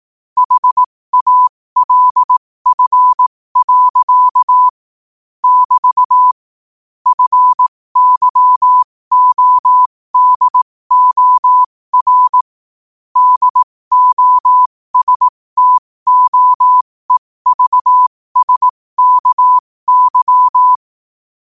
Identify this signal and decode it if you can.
Morse code, heard as a keyed tone, about 18 wpm, spaced as for 15 wpm, 1000 Hz. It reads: HALF. <BT> FYODOR DOSTOEVSKY
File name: qotdfiles / 15wpm